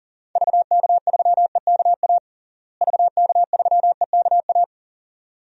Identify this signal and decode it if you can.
Morse code, heard as a keyed tone, about 40 wpm, 700 Hz. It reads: VK3EKA VK3EKA
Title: DXCC-100 Call Signs - 6 Characters - Repeated 2x - 40wpm